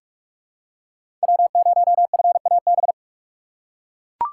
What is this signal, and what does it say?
Morse code, heard as a keyed tone, about 42 wpm, 700 Hz. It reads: W0FAB
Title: USA Call Signs - 5 Characters - 2x Character Spacing - 45wpm